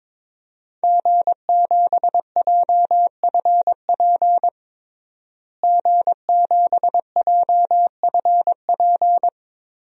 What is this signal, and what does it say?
Morse code, heard as a keyed tone, about 22 wpm, 700 Hz. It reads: G7JFP G7JFP
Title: DXCC-100 Call Signs - 5 Characters - Repeated 2x - 22wpm